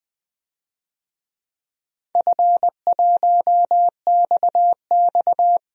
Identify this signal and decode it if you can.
Morse code, heard as a keyed tone, about 20 wpm, 700 Hz. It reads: F1XX
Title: DXCC-100 Call Signs - 4 Characters - 20wpm